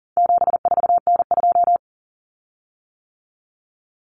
Morse code, heard as a keyed tone, about 40 wpm, 700 Hz. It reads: 74N2
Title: Common Cut Numbers - 4-Digit 40wpm